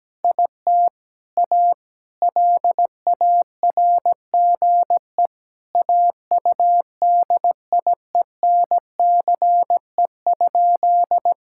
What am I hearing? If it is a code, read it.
Morse code, heard as a keyed tone, about 17 wpm, 700 Hz. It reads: IT A LARGE AUDIENCE?
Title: Sentences from Top 1000 Words - Review Part 5 - 17wpm